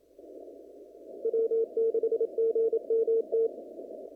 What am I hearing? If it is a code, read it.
Morse code, heard as a keyed tone, about 27 wpm, 450 Hz. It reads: W6GMT
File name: CW / kt5j wpx cw-op-k5pi